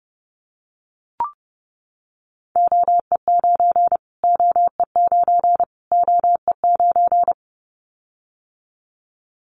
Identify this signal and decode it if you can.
Morse code, heard as a keyed tone, about 30 wpm, 700 Hz. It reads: OE9 OE9 OE9
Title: Callsign Prefixes for Top 25 DXCC - Repeated 3x - 30wpm